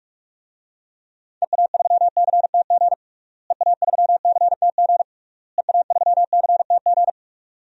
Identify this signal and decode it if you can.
Morse code, heard as a keyed tone, about 46 wpm, 700 Hz. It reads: EA3CTG EA3CTG EA3CTG
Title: DXCC-100 Call Signs - 6 Characters - Repeated 3x - 45wpm